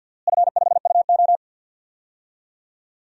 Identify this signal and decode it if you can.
Morse code, heard as a keyed tone, about 52 wpm, 700 Hz. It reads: F5UO
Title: DXCC-100 Call Signs - 4 Characters - Repeated 2x - 50wpm